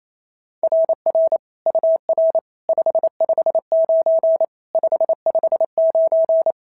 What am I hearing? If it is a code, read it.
Morse code, heard as a keyed tone, about 28 wpm, 690 Hz. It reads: RR UR 559 559
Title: SOTA QSO Practice 28wpm